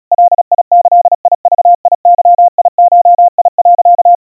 Morse code (Japanese Wabun code), heard as a keyed tone, about 37 wpm, 700 Hz. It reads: ガギグゲゴ、
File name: iroha 120 cpm 700 Hz Crispy